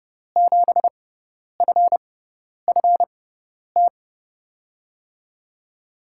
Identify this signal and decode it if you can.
Morse code, heard as a keyed tone, about 31 wpm, spaced as for 10 wpm, 700 Hz. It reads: 7FFT
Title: USA Call Signs - 6x Character Spacing - 30wpm